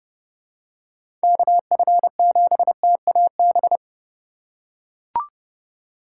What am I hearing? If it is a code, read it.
Morse code, heard as a keyed tone, about 30 wpm, 700 Hz. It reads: KF7TAB
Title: USA Call Signs - 6 Characters - 3x Character Spacing - 30wpm